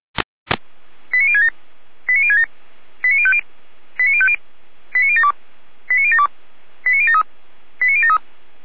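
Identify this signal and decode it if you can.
Five-tone selective call (ZVEI), heard as quick runs of five tones, tones of about 70 ms. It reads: 89067 89067 89050 89050 89072 89072 89073 89073
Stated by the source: ZVEI five-tone selective call received by SDR.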